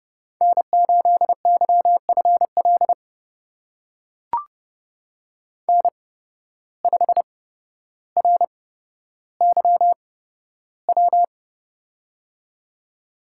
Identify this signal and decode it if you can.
Morse code, heard as a keyed tone, about 30 wpm, spaced as for 13 wpm, 700 Hz. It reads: N8YFL N5RYW
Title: USA Call Signs - 8x Character Spacing - 30wpm